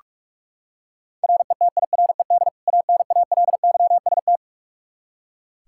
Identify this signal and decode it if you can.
Morse code, heard as a keyed tone, about 43 wpm, 700 Hz. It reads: RETIRED ANALYST
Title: K1USN SST QSO Practice 45wpm